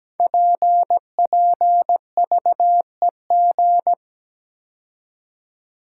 Morse code, heard as a keyed tone, about 17 wpm, 690 Hz. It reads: PPVEG
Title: Code Groups - 5 Letter:Numbers - 17wpm